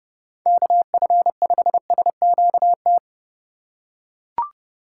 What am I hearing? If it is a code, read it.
Morse code, heard as a keyed tone, about 30 wpm, 700 Hz. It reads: KF5SQT
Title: USA Call Signs - 6 Characters - 4x Character Spacing - 30wpm